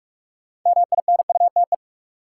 Morse code, heard as a keyed tone, about 42 wpm, 700 Hz. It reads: MINUTE
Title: Sentences from Top 500 Words - New Word Drill - 45wpm